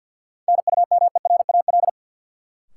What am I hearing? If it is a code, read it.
Morse code, heard as a keyed tone, about 52 wpm, 700 Hz. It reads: NUMERAL
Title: Sentences from Top 1000 Words - Review Part 2 - New Word Drill - 50wpm